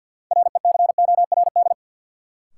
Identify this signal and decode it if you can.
Morse code, heard as a keyed tone, about 52 wpm, 700 Hz. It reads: RECORD
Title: Sentences from Top 500 Words - New Word Drill - 50wpm